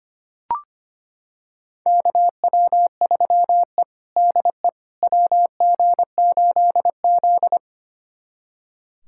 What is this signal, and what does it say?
Morse code, heard as a keyed tone, about 25 wpm, 700 Hz. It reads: KW3E DE WG8Z
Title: USA Callsign DE Callsign - 4 Characters - 25wpm